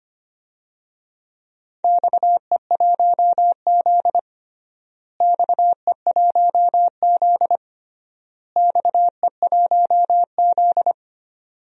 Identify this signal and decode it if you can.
Morse code, heard as a keyed tone, about 25 wpm, 700 Hz. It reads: XE1Z XE1Z XE1Z
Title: DXCC-100 Call Signs - 4 Characters - Repeated 3x - 25wpm